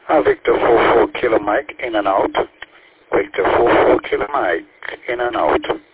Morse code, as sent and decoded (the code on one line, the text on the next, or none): none